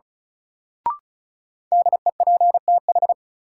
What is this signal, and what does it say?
Morse code, heard as a keyed tone, about 36 wpm, 700 Hz. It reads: DEPTH
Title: Sentences from Top 1200 Words - New Word Drill - 35wpm